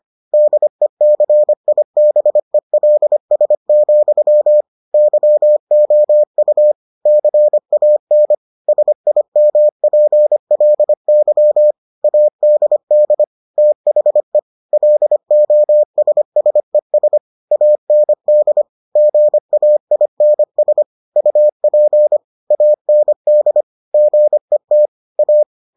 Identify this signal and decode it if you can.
Morse code, heard as a keyed tone, about 25 wpm, 600 Hz. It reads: DECIBELS, YOU CAN SIMPLY ADD THE LOSSES AND GAINS UP AND GET A